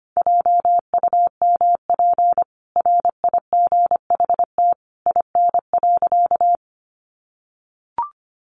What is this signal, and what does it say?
Morse code, heard as a keyed tone, about 25 wpm, 700 Hz. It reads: JUMP RIGHT IN.
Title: Sentences from Top 1300 Words 25wpm